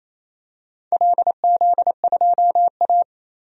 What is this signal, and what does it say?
Morse code, heard as a keyed tone, about 29 wpm, 700 Hz. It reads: LZ2A
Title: DXCC-100 Call Signs - 4 Characters - 28wpm